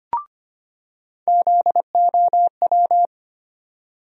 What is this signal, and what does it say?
Morse code, heard as a keyed tone, about 25 wpm, 700 Hz. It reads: ZOW
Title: USA Call Sign Postfixes - 3 Letters 25wpm